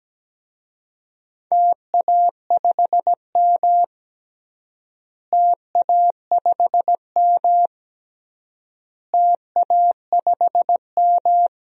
Morse code, heard as a keyed tone, about 17 wpm, 700 Hz. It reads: TA5M TA5M TA5M
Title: DXCC-100 Call Signs - 4 Characters - Repeated 3x - 17wpm